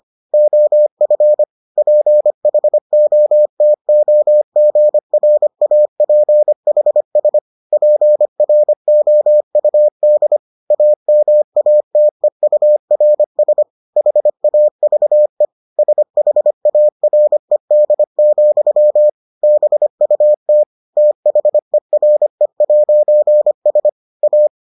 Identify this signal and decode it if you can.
Morse code, heard as a keyed tone, about 25 wpm, 600 Hz. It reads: OF PHOTOGRAPHS PROUD AMATEURS HAVE SHARED, BUT THERE'S A